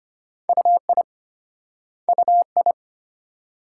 Morse code, heard as a keyed tone, about 32 wpm, 700 Hz. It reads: UI UI